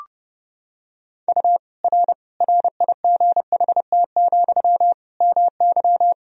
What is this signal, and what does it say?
Morse code, heard as a keyed tone, about 30 wpm, 700 Hz. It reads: U R RIGHT, MY
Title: Sentences from Top 400 Words and Abbreviations 30wpm